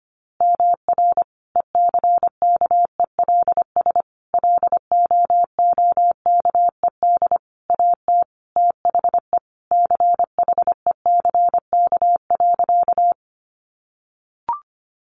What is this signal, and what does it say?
Morse code, heard as a keyed tone, about 25 wpm, 700 Hz. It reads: MR ECKELS LOOKED AT THE CHECK.
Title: A Sound Of Thunder 25wpm